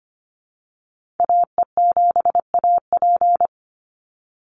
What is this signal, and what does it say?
Morse code, heard as a keyed tone, about 25 wpm, 700 Hz. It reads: AE7AP
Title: NA SOTA Activator Callsigns 25wpm